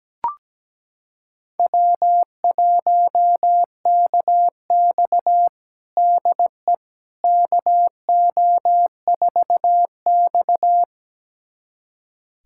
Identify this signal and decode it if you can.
Morse code, heard as a keyed tone, about 17 wpm, 700 Hz. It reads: W1KX DE KO4X
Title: USA Callsign DE Callsign - 4 Characters - 17wpm